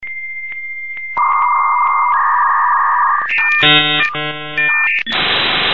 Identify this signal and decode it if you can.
Dial-up modem handshake.
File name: 56k-dialup-modem